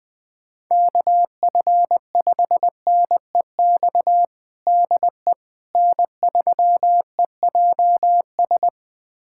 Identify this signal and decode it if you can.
Morse code, heard as a keyed tone, about 20 wpm, 700 Hz. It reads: KF5NEX DE N3EJS
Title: USA Callsign DE Callsign 20wpm